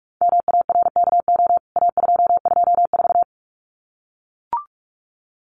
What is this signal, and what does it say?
Morse code, heard as a keyed tone, about 44 wpm, 700 Hz. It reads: MARKO A224
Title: CWT Exchange 45wpm